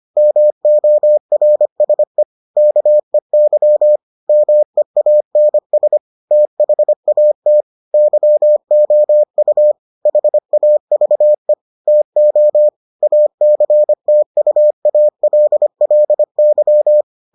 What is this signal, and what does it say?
Morse code, heard as a keyed tone, about 25 wpm, 600 Hz. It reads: MORSE KEY MEANS THAT YOU HAVE TO ACTUALLY